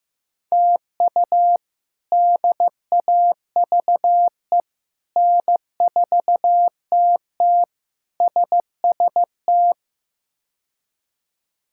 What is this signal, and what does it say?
Morse code, heard as a keyed tone, about 15 wpm, 700 Hz. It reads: TU DAVE N4TT SST
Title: K1USN SST QSO Practice - 15wpm